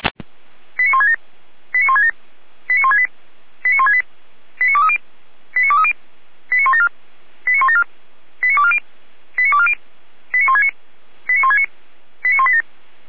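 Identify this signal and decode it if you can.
Five-tone selective call (ZVEI), heard as quick runs of five tones, tones of about 70 ms. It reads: 89167 89167 89168 89168 89230 89230 89174 89174 89240 89240 89169 89169 89187
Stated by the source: ZVEI five-tone selective call received by SDR.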